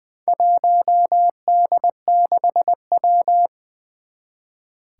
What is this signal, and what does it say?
Morse code, heard as a keyed tone, about 20 wpm, 700 Hz. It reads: 1D6W
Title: Four Challenging Letter-Number Combinations 20wpm